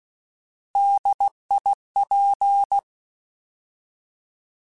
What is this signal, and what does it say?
Morse code, heard as a keyed tone, about 16 wpm, 800 Hz. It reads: DIP